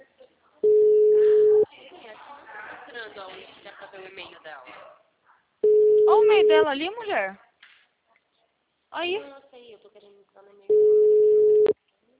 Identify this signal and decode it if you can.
Ringback tone (Europe, 425 Hz), with speech.
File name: out- -2029- - - .22985